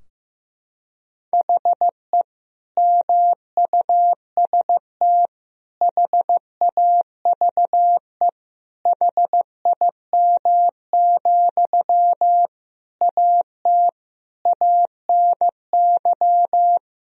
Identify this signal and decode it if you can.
Morse code, heard as a keyed tone, about 16 wpm, 700 Hz. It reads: HE MUST HAVE HIM, AT ANY
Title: Sentences from Top 1000 Words - Review Part 4 - 15wpm